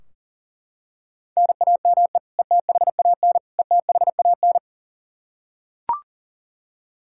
Morse code, heard as a keyed tone, about 40 wpm, 700 Hz. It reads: NAME ETHAN ETHAN
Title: Practice QSO - Exchange 1 - Beginning - 40wpm